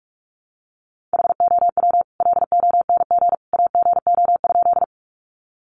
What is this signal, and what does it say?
Morse code, heard as a keyed tone, about 44 wpm, 700 Hz. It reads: HOW LONG AGO?
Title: Sentences from Top 500 Words 45wpm